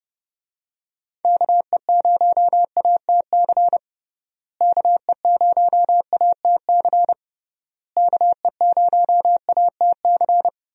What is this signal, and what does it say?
Morse code, heard as a keyed tone, about 31 wpm, 700 Hz. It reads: KE0ATC KE0ATC KE0ATC
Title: USA Call Signs - 6 Characters - Repeated 3x - 30wpm